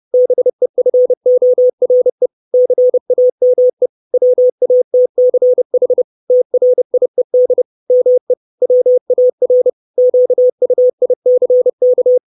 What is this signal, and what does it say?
Morse code, heard as a keyed tone, about 30 wpm, 500 Hz. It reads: BEFORE CAME WATCH TRIED ME WAR QUICK
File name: 632 Words 30WPM